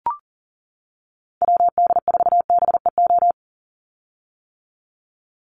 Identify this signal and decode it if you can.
Morse code, heard as a keyed tone, about 40 wpm, 700 Hz. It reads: WD4BEO
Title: USA Call Signs 40wpm